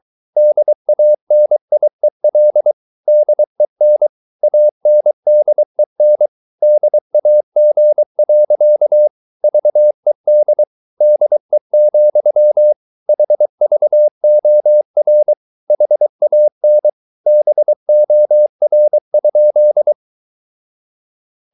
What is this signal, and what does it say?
Morse code, heard as a keyed tone, about 23 wpm, 600 Hz. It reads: DANIEL DEN ANDEN DAG. VED DE, HVOR HAN BOR?
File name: Ferie QRS 23wpm